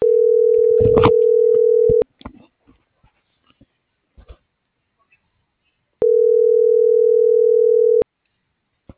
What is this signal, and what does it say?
Ringback tone (North America), 440 + 480 Hz.